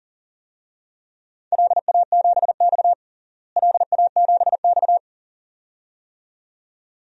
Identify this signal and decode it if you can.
Morse code, heard as a keyed tone, about 40 wpm, 700 Hz. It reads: LA7X LA7X
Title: DXCC-100 Call Signs - 4 Characters - Repeated 2x - 40wpm